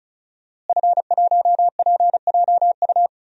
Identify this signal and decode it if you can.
Morse code, heard as a keyed tone, about 36 wpm, 700 Hz. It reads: F1PJU
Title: DXCC-100 Call Signs - 5 Characters - Repeated 2x - 35wpm